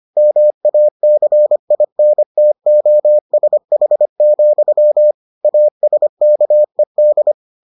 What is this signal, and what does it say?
Morse code, heard as a keyed tone, about 25 wpm, 600 Hz. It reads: MACINTOSH, ASKED